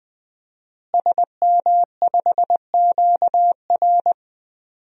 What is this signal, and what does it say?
Morse code, heard as a keyed tone, about 20 wpm, 700 Hz. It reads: SM5QR
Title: DXCC-100 Call Signs - 5 Characters - Repeated 3x - 20wpm